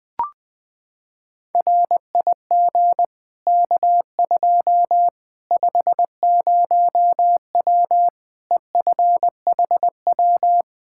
Morse code, heard as a keyed tone, about 20 wpm, 700 Hz. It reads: RIG K2 50W EFHW
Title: Practice QSO - Exchange 2 - Info - 20wpm